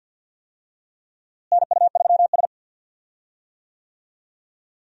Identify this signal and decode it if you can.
Morse code, heard as a keyed tone, about 49 wpm, 700 Hz. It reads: NU3S
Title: USA Call Signs - 4 Characters - Repeated 3x - 50wpm